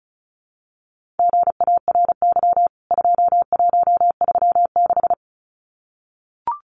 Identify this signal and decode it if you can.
Morse code, heard as a keyed tone, about 34 wpm, 700 Hz. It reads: GARY 2136
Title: CWT Exchange 35wpm